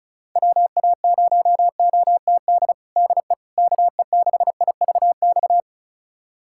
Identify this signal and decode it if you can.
Morse code, heard as a keyed tone, about 36 wpm, 700 Hz. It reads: WA0OTD DE KE6IVX
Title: USA Callsign DE Callsign 35wpm